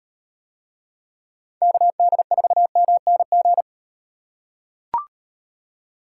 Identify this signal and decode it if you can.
Morse code, heard as a keyed tone, about 38 wpm, 700 Hz. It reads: KD4MNG